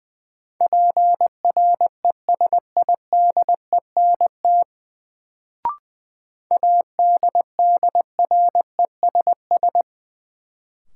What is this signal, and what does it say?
Morse code, heard as a keyed tone, about 20 wpm, 700 Hz. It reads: PRESIDENT ADDRESS
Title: Sentences from Top 1000 Words - New Word Drill - 20wpm